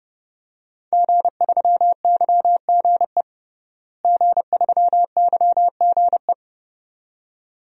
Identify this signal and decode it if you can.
Morse code, heard as a keyed tone, about 30 wpm, 700 Hz. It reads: G3YGE G3YGE
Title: DXCC-100 Call Signs - 5 Characters - Repeated 2x - 30wpm